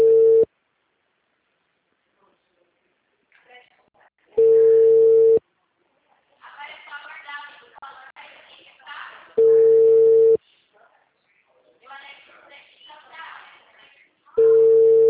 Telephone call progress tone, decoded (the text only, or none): ringback tone